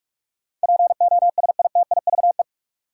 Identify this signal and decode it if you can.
Morse code, heard as a keyed tone, about 46 wpm, 700 Hz. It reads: POSITIVE
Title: Sentences from Top 1400 Words - New Word Drill Alt - 45wpm